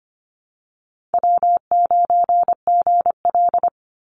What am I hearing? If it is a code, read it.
Morse code, heard as a keyed tone, about 25 wpm, 700 Hz. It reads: W9GL
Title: Easy Call Signs Repeated 2x - 25wpm